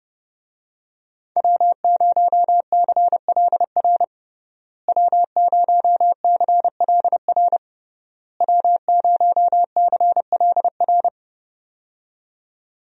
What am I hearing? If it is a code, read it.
Morse code, heard as a keyed tone, about 30 wpm, 700 Hz. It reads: W0CLR W0CLR W0CLR
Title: USA Call Signs - 5 Characters - Repeated 3x - 30wpm